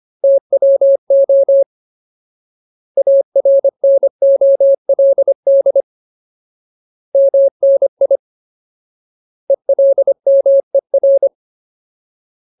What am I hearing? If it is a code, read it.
Morse code, heard as a keyed tone, about 25 wpm, 550 Hz. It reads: TWO ARNOLD MNI ELMER
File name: Ultimate Word List / 25WPM Ultimate